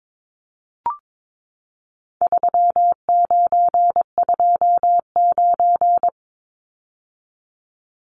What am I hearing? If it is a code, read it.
Morse code, heard as a keyed tone, about 22 wpm, 700 Hz. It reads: 3929
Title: Numbers - 4-Digit 22wpm